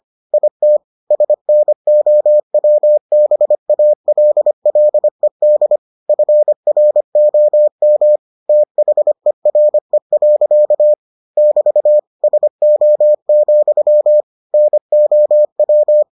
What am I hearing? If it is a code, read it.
Morse code, heard as a keyed tone, about 25 wpm, 600 Hz. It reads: IT SNOWBALLED FROM THERE. <BT> SO, NOW